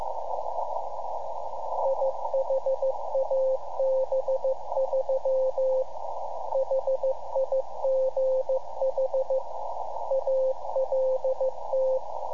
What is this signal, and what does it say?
Morse code, heard as a keyed tone, about 15 wpm, 530 Hz, with radio noise. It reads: IHAB3 HIGH ALT
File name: iHAB-3 / Repeater Recordings